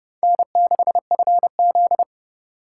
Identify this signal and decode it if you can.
Morse code, heard as a keyed tone, about 27 wpm, 700 Hz. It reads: N6FZ